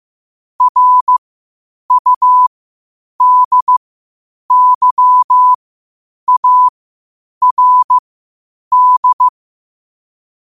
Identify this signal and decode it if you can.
Morse code, heard as a keyed tone, about 15 wpm, spaced as for 8 wpm, 1000 Hz. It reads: RUDYARD